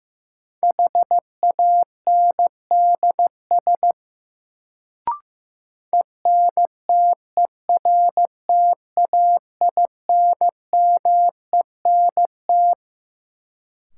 Morse code, heard as a keyed tone, about 15 wpm, 700 Hz. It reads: HANDS ENTERTAINMENT
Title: Sentences from Top 1000 Words - Review Part 6 - New Word Drill - 15wpm